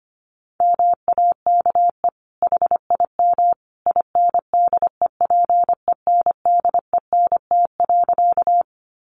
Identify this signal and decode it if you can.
Morse code, heard as a keyed tone, about 25 wpm, 700 Hz. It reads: MAKE HIM INDEPENDENT.
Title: Sentences from Top 900 Words 25wpm